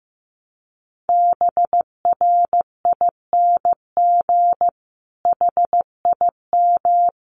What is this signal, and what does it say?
Morse code, heard as a keyed tone, about 15 wpm, 700 Hz. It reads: BRING HIM
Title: Sentences from Top 1300 Words 15wpm